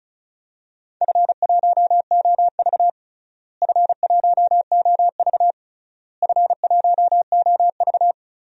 Morse code, heard as a keyed tone, about 36 wpm, 700 Hz. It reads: F1OV F1OV F1OV
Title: DXCC-100 Call Signs - 4 Characters - Repeated 3x - 35wpm